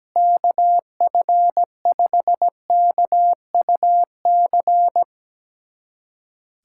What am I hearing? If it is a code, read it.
Morse code, heard as a keyed tone, about 17 wpm, 700 Hz. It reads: KF5KUC
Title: USA Callsign DE Callsign 17wpm